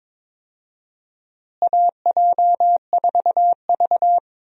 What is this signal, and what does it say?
Morse code, heard as a keyed tone, about 22 wpm, 700 Hz. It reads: AJ4V
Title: USA Call Signs - 4 Characters - Repeated 3x - 22wpm